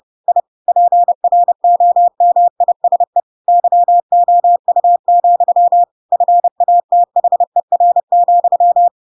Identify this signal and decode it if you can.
Morse code, heard as a keyed tone, about 30 wpm, 700 Hz. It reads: I PROMISE YOU, FATHER,
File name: Pinocchio / Pinocchio ch8 30 30 ik1hge